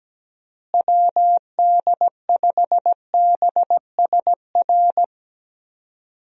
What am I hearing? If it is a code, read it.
Morse code, heard as a keyed tone, about 17 wpm, 700 Hz. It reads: WD5BSR
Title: USA Call Signs - 6 Characters - Repeated 3x - 17wpm